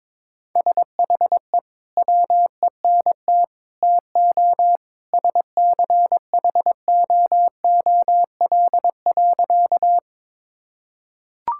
Morse code, heard as a keyed tone, about 22 wpm, 700 Hz. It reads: SHE WENT TO SCHOOL.
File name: practice / 22wpm / Sentences with 2-Word Phrases - 22wpm